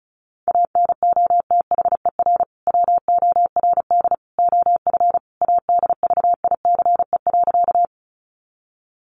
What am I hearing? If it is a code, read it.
Morse code, heard as a keyed tone, about 34 wpm, 700 Hz. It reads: ANOTHER WORD OF ADVICE.
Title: Sentences from Top 1100 Words 35wpm